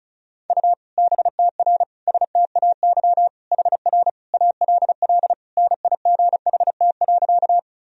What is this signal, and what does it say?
Morse code, heard as a keyed tone, about 34 wpm, 700 Hz. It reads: U BTR STAY HR ALL NIGHT.
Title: Sentences from Top 500 Words and Abbreviations 35wpm